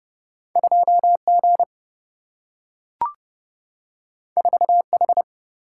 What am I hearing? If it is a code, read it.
Morse code, heard as a keyed tone, about 30 wpm, 700 Hz. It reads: 2G 4H
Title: Single-Letter Single-Number 30wpm